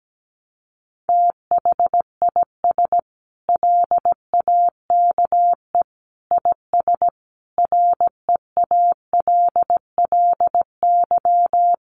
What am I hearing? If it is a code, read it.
Morse code, heard as a keyed tone, about 17 wpm, 700 Hz. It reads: THIS LAKE IS REALLY
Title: Sentences from Top 1100 Words 17wpm